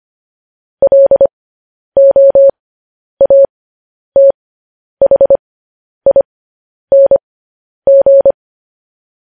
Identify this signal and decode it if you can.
Morse code, heard as a keyed tone, about 25 wpm, spaced as for 10 wpm, 550 Hz. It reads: LOATHING
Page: News Headlines in Morse Code at 10 WPM